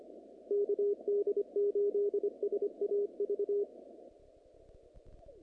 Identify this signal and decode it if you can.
Morse code, heard as a keyed tone, about 24 wpm, 410 Hz. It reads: KD8SAV
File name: contests / naqp-cw-k5tr-op-k5pi- -2200